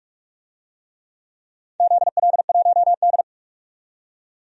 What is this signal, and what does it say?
Morse code, heard as a keyed tone, about 47 wpm, 700 Hz. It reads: ZL1D
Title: DXCC-100 Call Signs - 4 Characters - 45wpm